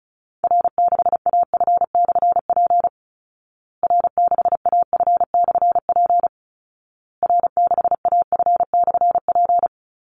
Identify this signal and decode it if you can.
Morse code, heard as a keyed tone, about 34 wpm, 700 Hz. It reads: R6AF/P R6AF/P R6AF/P
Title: Call Signs Repeated 2x - 35wpm